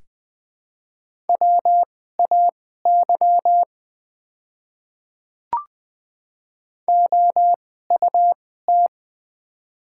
Morse code, heard as a keyed tone, about 20 wpm, spaced as for 14 wpm, 700 Hz. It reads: WAY OUT
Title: 2x Character Spacing - Top 100 Words - 20wpm